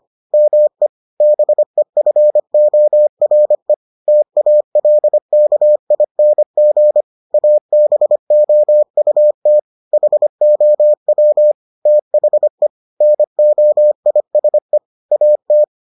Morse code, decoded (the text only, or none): ME BEFORE TALKING ABOUT HOW THE NOISE AT